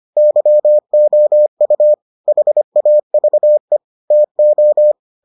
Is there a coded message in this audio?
Yes: Morse code from 0.2 to 4.9 s, heard as a keyed tone, about 25 wpm, 600 Hz. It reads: YOU HAVE TO